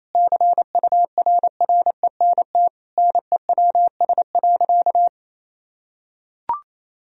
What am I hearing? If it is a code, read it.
Morse code, heard as a keyed tone, about 28 wpm, 700 Hz. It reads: CURRENT NEWS.
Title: Sentences from Top 1000 Words - Review Part 4 - 28wpm